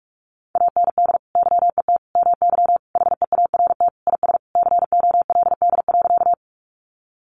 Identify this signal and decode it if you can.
Morse code, heard as a keyed tone, about 44 wpm, 700 Hz. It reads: AND YET MY HEART IS COLD.
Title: Sentences from Top 1200 Words 45wpm